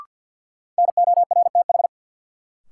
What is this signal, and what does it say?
Morse code, heard as a keyed tone, about 48 wpm, 700 Hz. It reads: NORTH
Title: Sentences from Top 400 Words - New Word Drill - 50wpm